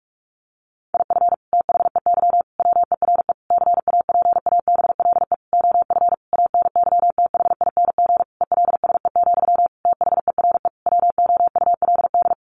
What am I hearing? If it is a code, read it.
Morse code, heard as a keyed tone, about 44 wpm, 700 Hz. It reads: IF THEY WERE CAPABLE OF ANYTHING ELSE, THERE WOULD
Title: Sentences from Top 1200 Words 45wpm